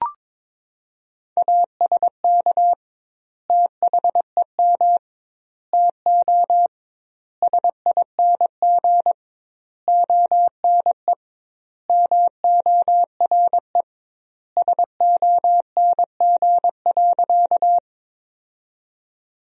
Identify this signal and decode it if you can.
Morse code, heard as a keyed tone, about 22 wpm, 700 Hz. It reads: ASK THEM TO SING ONE MORE SONG.
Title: 2x Word-Spacing - Sentences from Top 400 Words - 22wpm